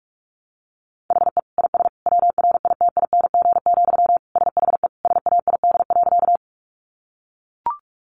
Morse code, heard as a keyed tone, about 44 wpm, 700 Hz. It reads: HE IS WRITING, SHE SAID.
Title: Sentences from Top 800 Words 45wpm